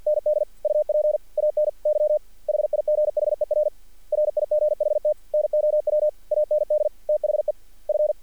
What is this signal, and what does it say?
Morse code, heard as a keyed tone, about 48 wpm, 600 Hz. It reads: ND UP ANY HIGHER RIGHT NOW AND THE F
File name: archived / CRSnet